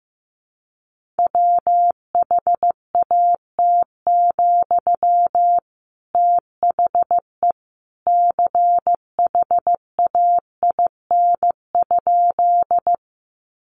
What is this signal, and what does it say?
Morse code, heard as a keyed tone, about 15 wpm, 700 Hz. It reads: WHAT, THE CHAIN?
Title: Sentences from Top 1500 Words 15wpm